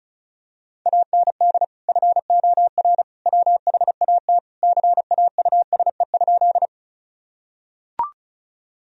Morse code, heard as a keyed tone, about 34 wpm, 700 Hz. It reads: AND FOR WHAT CAUSE?
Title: Sentences from Top 1000 Words - Review Part 1 - 35wpm